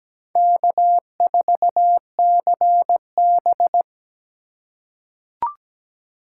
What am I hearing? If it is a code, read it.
Morse code, heard as a keyed tone, about 17 wpm, 700 Hz. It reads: K4CB
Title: USA Call Signs - 3x Character Spacing - 17wpm